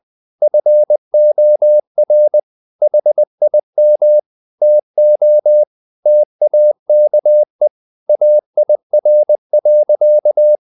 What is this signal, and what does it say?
Morse code, heard as a keyed tone, about 20 wpm, 600 Hz. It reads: FOR HIM TO TAKE AIR.
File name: the old man and the sea part 2 20wpm